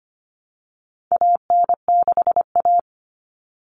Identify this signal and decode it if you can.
Morse code, heard as a keyed tone, about 25 wpm, 700 Hz. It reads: AN6A
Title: All Cut Numbers - 4-Digit 25wpm